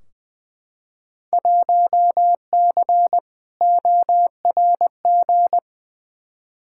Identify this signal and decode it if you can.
Morse code, heard as a keyed tone, about 20 wpm, 700 Hz. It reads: 1C ORG
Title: Field Day Exchanges - 20wpm